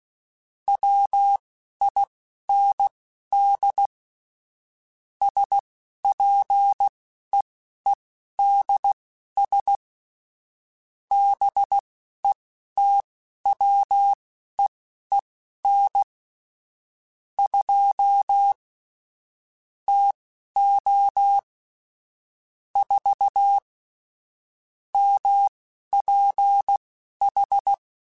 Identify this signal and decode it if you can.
Morse code, heard as a keyed tone, about 16 wpm, spaced as for 12 wpm, 800 Hz. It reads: WIND SPEEDS BETWEEN 2 TO 4 MPH